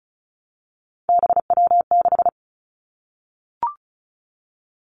Morse code, heard as a keyed tone, about 36 wpm, 700 Hz. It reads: BW6
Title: Three Challenging Letter-Number Combinations 35wpm